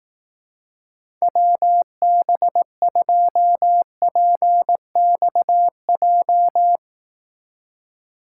Morse code, heard as a keyed tone, about 18 wpm, 700 Hz. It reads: WB2PXJ
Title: Speed Racing - USA 2x3 Call Signs 15wpm